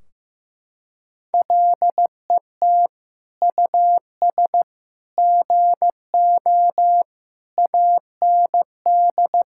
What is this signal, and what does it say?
Morse code, heard as a keyed tone, about 15 wpm, 700 Hz. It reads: LET US GO AND
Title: Sentences from Top 1600 Words 15wpm